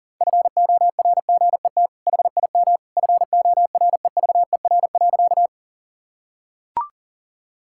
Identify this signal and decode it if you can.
Morse code, heard as a keyed tone, about 40 wpm, 700 Hz. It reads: FORGET HIM FOREVER.
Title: Sentences from Top 1600 Words 40wpm